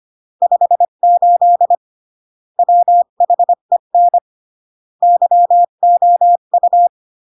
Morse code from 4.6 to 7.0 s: YOU